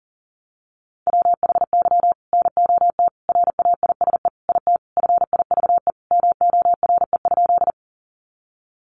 Morse code, heard as a keyed tone, about 40 wpm, 700 Hz. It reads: WHY NOT RAISE IT FIVE MORE?
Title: Sentences from Top 1400 Words 40wpm